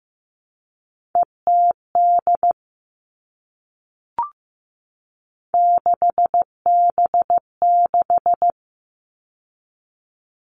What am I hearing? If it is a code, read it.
Morse code, heard as a keyed tone, about 15 wpm, 700 Hz. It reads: ETD 6B6
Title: All Cut Numbers - 3-Digit 15wpm